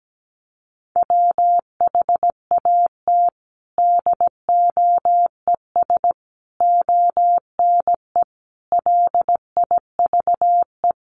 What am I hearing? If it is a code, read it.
Morse code, heard as a keyed tone, about 17 wpm, 700 Hz. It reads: WHAT DOES ONE LIVE
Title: Sentences from Top 200 Words 17wpm